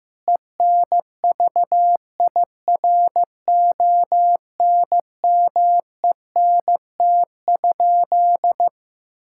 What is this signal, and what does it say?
Morse code, heard as a keyed tone, about 15 wpm, 700 Hz. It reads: ENVIRONMENT?
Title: Sentences from Top 1000 Words - Review Part 5 - 15wpm